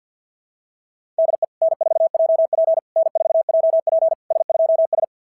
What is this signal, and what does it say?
Morse code, heard as a keyed tone, about 46 wpm, 660 Hz. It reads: DE N4JP N4JP S2S
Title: SOTA QSO Practice 50wpm